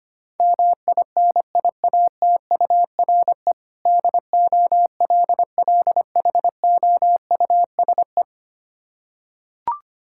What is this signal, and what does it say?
Morse code, heard as a keyed tone, about 25 wpm, 700 Hz. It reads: MINIATURE DOLLHOUSE
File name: practice / 25wpm / Difficult Words with Hints - 25wpm